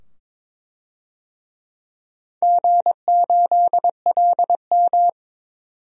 Morse code, heard as a keyed tone, about 22 wpm, 700 Hz. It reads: G8LM
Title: DXCC-100 Call Signs - 4 Characters - 22wpm